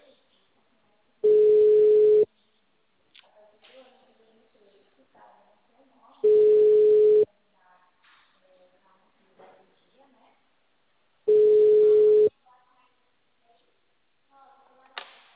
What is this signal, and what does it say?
Ringback tone (Europe, 425 Hz), with speech.
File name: out- -2017- - - .67859